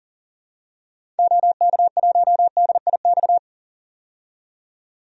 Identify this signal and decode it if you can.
Morse code, heard as a keyed tone, about 40 wpm, 700 Hz. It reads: OK1DIX
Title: DXCC-100 Call Signs - 6 Characters - 40wpm